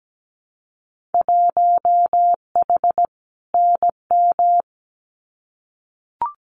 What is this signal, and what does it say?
Morse code, heard as a keyed tone, about 17 wpm, 700 Hz. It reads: 1H NM
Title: Winter Field Day Exchanges 17wpm